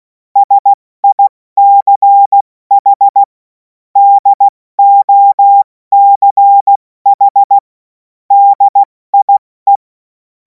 Morse code, heard as a keyed tone, about 16 wpm, 800 Hz. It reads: SICH DOCH DIE
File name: Morsekurs / Witze / Witz 11. August 2026 70 F80